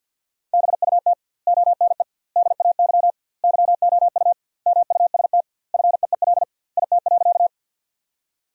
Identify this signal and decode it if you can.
Morse code, heard as a keyed tone, about 50 wpm, 700 Hz. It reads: BUT ONE DAY YOU MUST FEEL IT.
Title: 2x Word-Spacing - Sentences from Top 400 Words - 50wpm